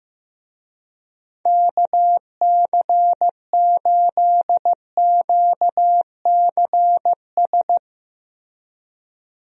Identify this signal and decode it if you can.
Morse code, heard as a keyed tone, about 15 wpm, 700 Hz. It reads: KC8QCS
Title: USA Call Signs - 6 Characters - Repeated 3x - 15wpm